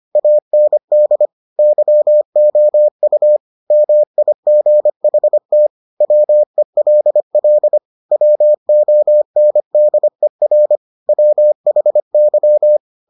Morse code, heard as a keyed tone, about 25 wpm, 600 Hz. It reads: AND YOU MIGHT WELL WONDER WHY